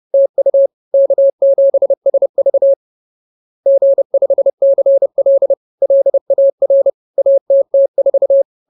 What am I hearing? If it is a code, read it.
Morse code, heard as a keyed tone, about 30 wpm, 550 Hz. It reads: TU K7SV G5CL LAR ATT4
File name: CWT-216-30